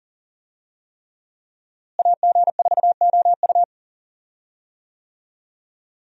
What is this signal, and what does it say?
Morse code, heard as a keyed tone, about 40 wpm, 700 Hz. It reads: AG4OU
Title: USA Call Signs - 5 Characters - Repeated 3x - 40wpm